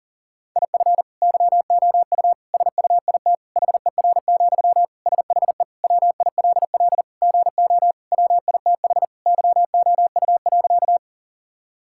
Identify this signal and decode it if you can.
Morse code, heard as a keyed tone, about 40 wpm, 700 Hz. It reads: IF YOU SUIT HER, SHE WILL GO WITH YOU.
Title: Sentences from Top 1600 Words 40wpm